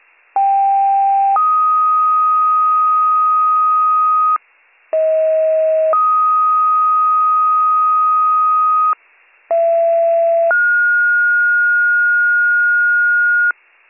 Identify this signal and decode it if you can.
Two-tone page, heard as a one-second tone, then a three-second tone, about 790 then 1250 Hz, about 640 then 1250 Hz, about 670 then 1430 Hz. It is heard from 0.4 to 13.5 s.